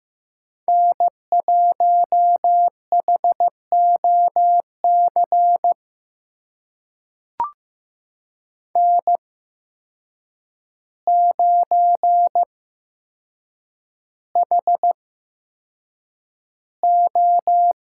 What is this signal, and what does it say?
Morse code, heard as a keyed tone, about 15 wpm, spaced as for 7 wpm, 700 Hz. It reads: N1HOC N9HO
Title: USA Call Signs - 5 Characters - 8x Character Spacing - 15wpm